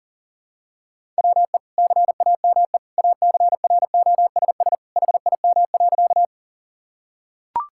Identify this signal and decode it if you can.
Morse code, heard as a keyed tone, about 40 wpm, 700 Hz. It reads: WE CAME ACROSS HIM.
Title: Sentences from Top 1000 Words - Review Part 5 - 40wpm